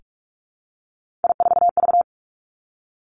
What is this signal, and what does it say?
Morse code, heard as a keyed tone, about 46 wpm, 700 Hz. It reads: I4V
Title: Three Challenging Letter-Number Combinations 45wpm